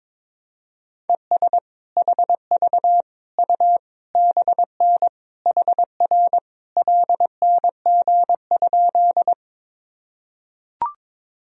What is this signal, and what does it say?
Morse code, heard as a keyed tone, about 22 wpm, 700 Hz. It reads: ES HV U BN HR LNG?
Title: Sentences from Top 200 Words and Abbreviations 22wpm